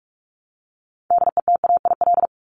Morse code, heard as a keyed tone, about 44 wpm, 700 Hz. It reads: DETAIL
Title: Top 1251-1500 Words 45wpm